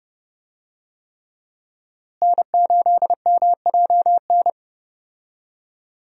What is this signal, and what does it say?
Morse code, heard as a keyed tone, about 30 wpm, 700 Hz. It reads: N8MJN
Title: USA Call Signs - 5 Characters - Repeated 3x - 30wpm